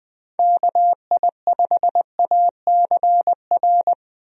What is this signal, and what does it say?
Morse code, heard as a keyed tone, about 21 wpm, 700 Hz. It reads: KI5ACR
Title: USA Call Signs - 6 Characters - Repeated 2x - 20wpm